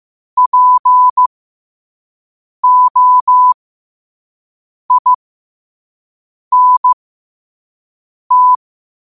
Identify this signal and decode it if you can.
Morse code, heard as a keyed tone, about 15 wpm, spaced as for 5 wpm, 1000 Hz. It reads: POINT
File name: qotdfiles / 5wpm